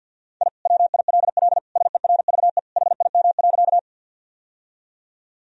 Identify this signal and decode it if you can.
Morse code, heard as a keyed tone, about 52 wpm, 700 Hz. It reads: I WILL SERVE HIM.
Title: Sentences from Top 1000 Words - Review Part 2 - 50wpm